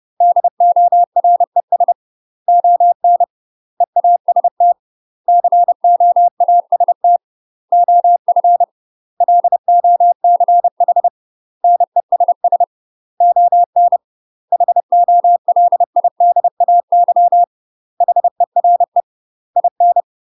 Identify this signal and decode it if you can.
Morse code, heard as a keyed tone, about 30 wpm, 700 Hz. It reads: DORES ON EAST COAST OF LOCH NESS ON HOLIDAY HERE IN
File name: UK MASTER QSOs TRAINING / 30WPM / UK MASTER QSOs 0043